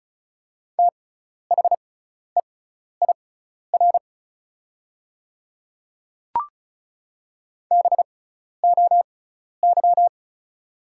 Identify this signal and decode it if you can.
Morse code, heard as a keyed tone, about 34 wpm, spaced as for 12 wpm, 690 Hz. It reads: THEIR BOY
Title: 6x Character Spacing - Top 200 Words - 35wpm